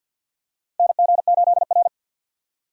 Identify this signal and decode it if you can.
Morse code, heard as a keyed tone, about 51 wpm, 700 Hz. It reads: NG8R
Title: USA Call Signs - 4 Characters - 50wpm